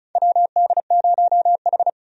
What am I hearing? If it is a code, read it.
Morse code, heard as a keyed tone, about 34 wpm, 700 Hz. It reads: WD0H
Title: USA Call Signs - 4 Characters - 35wpm